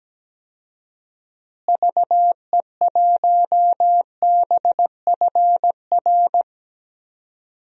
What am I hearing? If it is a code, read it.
Morse code, heard as a keyed tone, about 17 wpm, 700 Hz. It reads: VE1BFR
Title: DXCC-100 Call Signs - 6 Characters - Repeated 2x - 17wpm